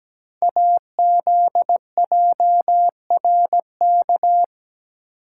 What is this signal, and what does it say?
Morse code, heard as a keyed tone, about 17 wpm, 700 Hz. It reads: AZJRK
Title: Code Groups - 5 Letter:Numbers - 17wpm